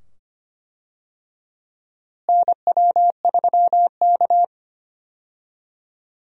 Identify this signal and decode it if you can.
Morse code, heard as a keyed tone, about 25 wpm, 700 Hz. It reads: NW3K
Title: USA Call Signs - 4 Characters - Repeated 2x - 25wpm